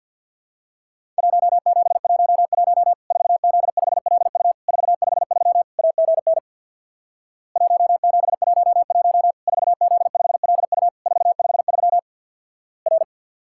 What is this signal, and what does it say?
Morse code, heard as a keyed tone, about 50 wpm, 700 Hz. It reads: 1711 4Z5LU 453 AGN 1711 4Z5LU 453 R
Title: Worked All Europe DX Contest - QTC Exchange with Repeats - 50wpm